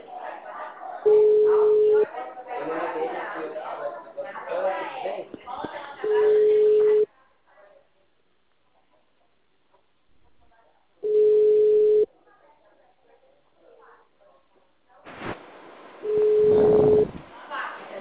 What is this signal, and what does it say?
Ringback tone (Europe, 425 Hz), with speech.